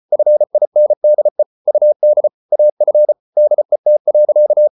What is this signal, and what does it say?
Morse code, heard as a keyed tone, about 34 wpm, 600 Hz. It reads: FINDE UD AF DET.
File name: Ferie QRQ 34wpm